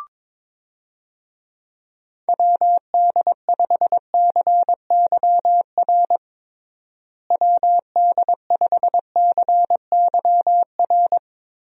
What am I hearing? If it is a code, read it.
Morse code, heard as a keyed tone, about 21 wpm, 700 Hz. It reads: WD5CYR WD5CYR
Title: USA Call Signs - 6 Characters - Repeated 3x - 22wpm